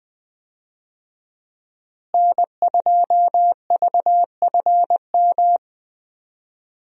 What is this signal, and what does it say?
Morse code, heard as a keyed tone, about 20 wpm, 700 Hz. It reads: N2VFM
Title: USA Call Signs - 5 Characters - Repeated 3x - 20wpm